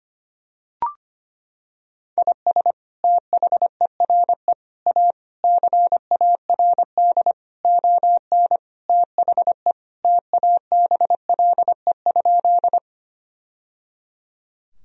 Morse code, heard as a keyed tone, about 25 wpm, 700 Hz. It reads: IS THERE A CARD ON THE TABLE?
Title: Sentences from Top 1000 Words - Review Part 4 - 25wpm